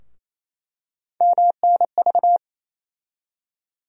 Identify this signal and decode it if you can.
Morse code, heard as a keyed tone, about 28 wpm, 700 Hz. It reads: MNV
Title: USA Call Sign Postfixes - 3 Letters 28wpm